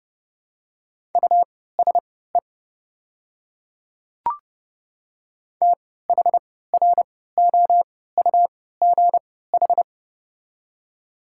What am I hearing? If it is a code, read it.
Morse code, heard as a keyed tone, about 30 wpm, spaced as for 17 wpm, 690 Hz. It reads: USE THROUGH
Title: 3x Character Spacing - Top 500 Words - 30wpm